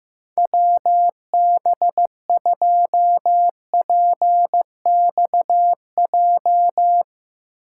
Morse code, heard as a keyed tone, about 16 wpm, 700 Hz. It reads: WB2PXJ
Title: Speed Racing - USA 2x3 Call Signs 15wpm